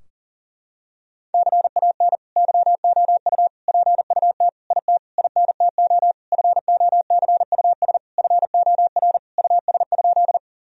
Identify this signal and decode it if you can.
Morse code, heard as a keyed tone, about 40 wpm, 700 Hz. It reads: CAN YOU PUT IT INTO FOCUS FOR US?
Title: Sentences from Top 1000 Words - Review Part 4 - 40wpm